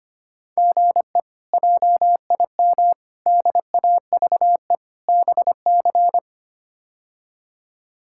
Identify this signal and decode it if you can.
Morse code, heard as a keyed tone, about 25 wpm, 700 Hz. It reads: GE JIM DAVE BC
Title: K1USN SST QSO Practice - 25wpm